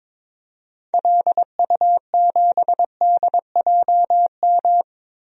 Morse code, heard as a keyed tone, about 22 wpm, 700 Hz. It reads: LU7DJM
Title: DXCC-100 Call Signs - 6 Characters - Repeated 2x - 22wpm